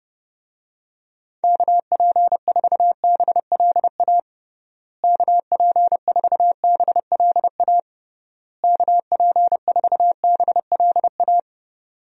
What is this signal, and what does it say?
Morse code, heard as a keyed tone, about 30 wpm, 700 Hz. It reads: KP4BLA KP4BLA KP4BLA
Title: DXCC-100 Call Signs - 6 Characters - Repeated 3x - 30wpm